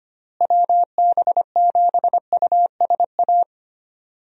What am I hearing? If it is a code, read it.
Morse code, heard as a keyed tone, about 25 wpm, 700 Hz. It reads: WB7USA
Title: USA Call Signs - 6 Characters - Repeated 3x - 25wpm